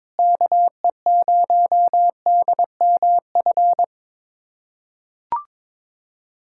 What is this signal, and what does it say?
Morse code, heard as a keyed tone, about 22 wpm, 700 Hz. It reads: KE0DMF
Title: USA Callsign DE Callsign 22wpm